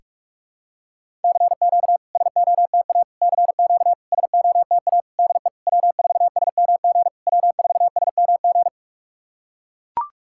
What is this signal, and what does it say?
Morse code, heard as a keyed tone, about 44 wpm, 690 Hz. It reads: CQ SOTA CQ SOTA DE W4SMG W4SMG
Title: SOTA QSO Practice 45wpm